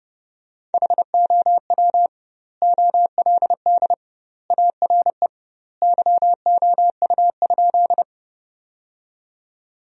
Morse code, heard as a keyed tone, about 32 wpm, 700 Hz. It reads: HOW OLD ARE YOU?
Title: 2x Word-Spacing - Sentences from Top 200 Words - 30wpm